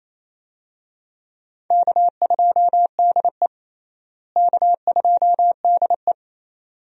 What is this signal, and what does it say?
Morse code, heard as a keyed tone, about 28 wpm, 700 Hz. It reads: K2DE K2DE
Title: USA Call Signs - 4 Characters - Repeated 2x - 28wpm